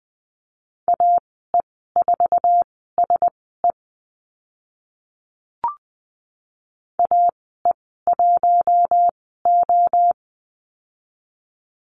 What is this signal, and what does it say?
Morse code, heard as a keyed tone, about 20 wpm, spaced as for 15 wpm, 700 Hz. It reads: AE4SE AE1O
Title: 062 Lesson 11g - Letter h - Call Signs 20wpm